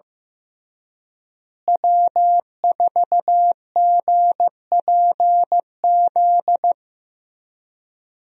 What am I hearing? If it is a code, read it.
Morse code, heard as a keyed tone, about 15 wpm, 700 Hz. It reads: W4GPZ
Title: USA Call Signs - 5 Characters - Repeated 3x - 15wpm